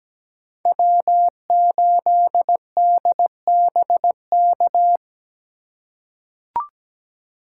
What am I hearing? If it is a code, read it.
Morse code, heard as a keyed tone, about 17 wpm, 700 Hz. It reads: W8DBK
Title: USA Call Signs - 5 Characters - 4x Character Spacing - 17wpm